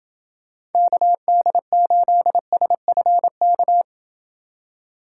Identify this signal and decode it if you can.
Morse code, heard as a keyed tone, about 29 wpm, 700 Hz. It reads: KD8SFK